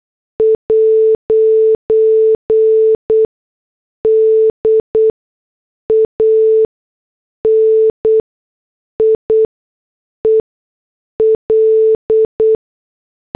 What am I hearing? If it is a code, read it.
Morse code, heard as a keyed tone, about 8 wpm, spaced as for 6 wpm, 440 Hz. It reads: 'DANIEL
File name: morse / headlines